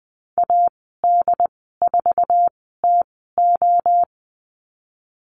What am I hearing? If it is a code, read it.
Morse code, heard as a keyed tone, about 19 wpm, spaced as for 14 wpm, 700 Hz. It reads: AD4TO
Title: 088 Lesson 15g - Number 5 - Call Signs 20wpm